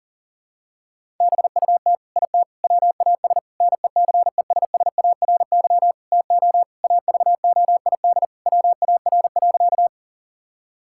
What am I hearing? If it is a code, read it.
Morse code, heard as a keyed tone, about 40 wpm, 700 Hz. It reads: BUT IT WAS NECESSARY TO AVOID WAR.
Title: Sentences from Top 1000 Words - Review Part 6 - 40wpm